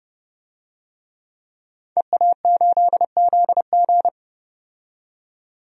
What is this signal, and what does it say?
Morse code, heard as a keyed tone, about 29 wpm, 700 Hz. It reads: EA8ZG
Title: DXCC-100 Call Signs - 5 Characters - 30wpm